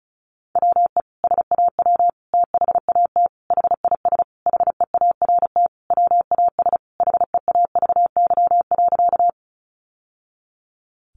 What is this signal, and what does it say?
Morse code, heard as a keyed tone, about 34 wpm, 700 Hz. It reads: WE SAW THAT HIS HEART WAS HEAVY.
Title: Sentences from Top 1200 Words 35wpm